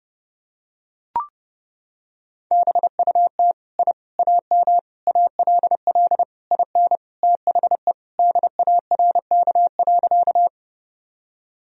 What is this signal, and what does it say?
Morse code, heard as a keyed tone, about 30 wpm, 700 Hz. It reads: BUT I AM ALL IN THE DARK.
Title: Sentences from Top 1000 Words - Review Part 2 - 30wpm